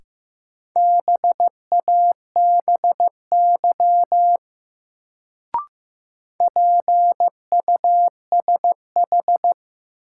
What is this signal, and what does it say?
Morse code, heard as a keyed tone, about 15 wpm, 700 Hz. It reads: BABY PUSH
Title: Sentences from Top 1300 Words - New Word Drill Alt - 15wpm